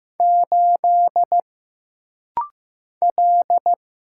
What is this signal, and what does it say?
Morse code, heard as a keyed tone, about 15 wpm, 700 Hz. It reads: 8 L
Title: Single Letter-Number Alt - Rapid-Fire 15wpm